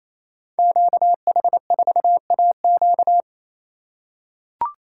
Morse code, heard as a keyed tone, about 27 wpm, 700 Hz. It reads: QH4AQ
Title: Code Groups - 5 Letter:Numbers - 28wpm